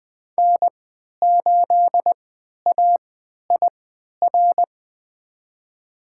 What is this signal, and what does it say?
Morse code, heard as a keyed tone, about 20 wpm, spaced as for 11 wpm, 700 Hz. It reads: N8AIR
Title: USA Call Signs - 5 Characters - 3x Character Spacing - 20wpm